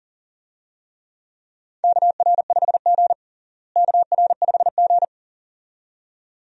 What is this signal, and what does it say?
Morse code, heard as a keyed tone, about 40 wpm, 700 Hz. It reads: KR5G KR5G
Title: USA Call Signs - 4 Characters - Repeated 2x - 40wpm